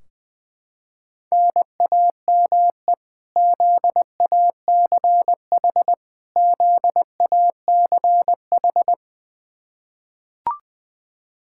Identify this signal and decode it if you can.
Morse code, heard as a keyed tone, about 21 wpm, 700 Hz. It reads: NAME ZACH ZACH
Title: Practice QSO - Exchange 1 - Beginning - 20wpm